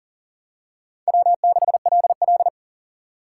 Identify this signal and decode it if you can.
Morse code, heard as a keyed tone, about 40 wpm, 700 Hz. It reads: W6LL
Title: USA Call Signs - 4 Characters - 3x Character Spacing - 40wpm